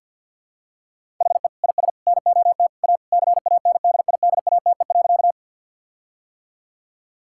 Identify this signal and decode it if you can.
Morse code, heard as a keyed tone, about 50 wpm, 700 Hz. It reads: HE IS NOT A CANDIDATE.
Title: Sentences from Top 1600 Words 50wpm